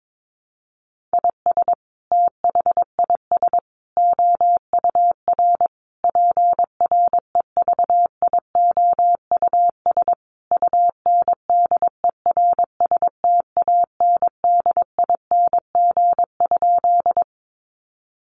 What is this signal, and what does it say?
Morse code, heard as a keyed tone, about 22 wpm, 700 Hz. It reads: IS THIS OUR PREVIOUS UNDERSTANDING?
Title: Sentences from Top 1500 Words 22wpm